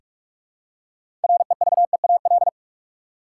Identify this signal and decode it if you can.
Morse code, heard as a keyed tone, about 42 wpm, 700 Hz. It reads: REVEAL
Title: Six Letter Words - 45wpm